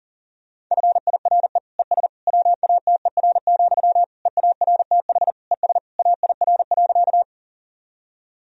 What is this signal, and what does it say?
Morse code, heard as a keyed tone, about 40 wpm, 700 Hz. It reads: FIRE ES WATER, EARTH ES AIR.
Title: Sentences from Top 400 Words and Abbreviations 40wpm